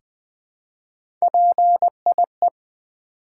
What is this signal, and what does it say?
Morse code, heard as a keyed tone, about 20 wpm, 700 Hz. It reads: PIE
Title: Three Letter Words - 20wpm